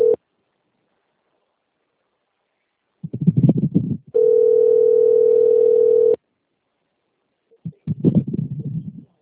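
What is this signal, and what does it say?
Ringback tone (North America), 440 + 480 Hz.
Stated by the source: Outgoing telephone call recording from the Presidente Prudente 4 system.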